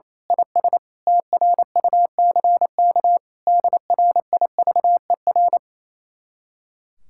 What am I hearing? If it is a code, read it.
Morse code, heard as a keyed tone, about 26 wpm, 700 Hz. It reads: IS TRUCK DRIVER
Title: Speed-Racing Common QSO Phrases Encore - 28wpm